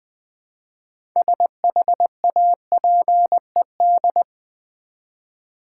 Morse code, heard as a keyed tone, about 20 wpm, 700 Hz. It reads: SHAPED
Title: 6 Letter Words - First 3 Letters Given - 20wpm